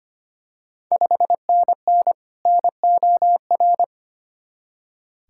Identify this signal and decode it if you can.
Morse code, heard as a keyed tone, about 25 wpm, 700 Hz. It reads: 5NN NOR
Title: Ontario QSO Party Exchange 25wpm